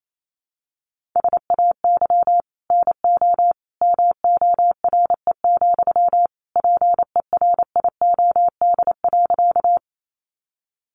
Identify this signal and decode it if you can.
Morse code, heard as a keyed tone, about 28 wpm, 700 Hz. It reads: SAY NO MORE, PERIOD.
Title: Sentences from Top 800 Words 28wpm